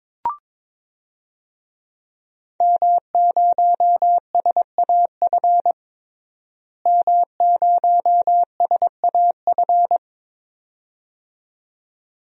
Morse code, heard as a keyed tone, about 22 wpm, 700 Hz. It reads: M0SAF M0SAF
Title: DXCC-100 Call Signs - 5 Characters - Repeated 2x - 22wpm